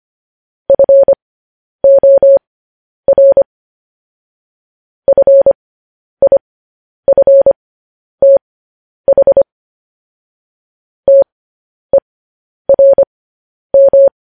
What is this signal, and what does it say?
Morse code, heard as a keyed tone, about 25 wpm, spaced as for 10 wpm, 550 Hz. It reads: FOR FIFTH TERM
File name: podcast / News headlines-1x-25x10